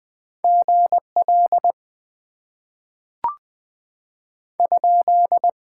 Morse code, heard as a keyed tone, about 20 wpm, 700 Hz. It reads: GL ?
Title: Top 90 QSO Elements - 6x Character Spacing - 20wpm